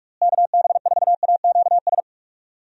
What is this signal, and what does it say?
Morse code, heard as a keyed tone, about 44 wpm, 700 Hz. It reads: KB4AQS
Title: USA Call Signs - 6 Characters - Repeated 2x - 45wpm